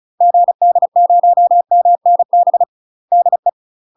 Morse code, heard as a keyed tone, about 34 wpm, 700 Hz. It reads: GD0MNB DE
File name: UK MASTER QSOs TRAINING / 35WPM / UK MASTER QSOs 0037